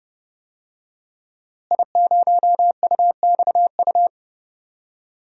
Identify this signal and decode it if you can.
Morse code, heard as a keyed tone, about 30 wpm, 700 Hz. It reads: I0UXU
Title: DXCC-100 Call Signs - 5 Characters - Repeated 2x - 30wpm